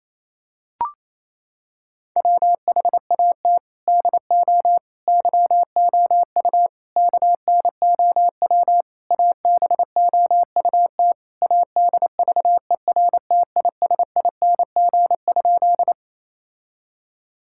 Morse code, heard as a keyed tone, about 27 wpm, 700 Hz. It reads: WHAT DO YOU KNOW ABOUT ADVERTISING?
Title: Sentences from Top 1000 Words - Review Part 6 - 28wpm